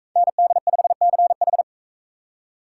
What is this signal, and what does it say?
Morse code, heard as a keyed tone, about 41 wpm, 700 Hz. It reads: ND5CH